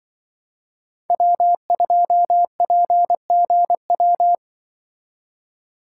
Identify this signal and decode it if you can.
Morse code, heard as a keyed tone, about 24 wpm, 700 Hz. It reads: W2PGW
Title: Speed Racing - USA 1x3 Call Signs 20wpm